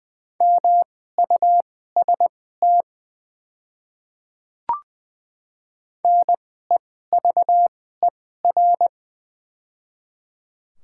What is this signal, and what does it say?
Morse code, heard as a keyed tone, about 20 wpm, spaced as for 15 wpm, 700 Hz. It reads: MUST NEVER
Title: 2x Character Spacing - Top 300 Words - 20wpm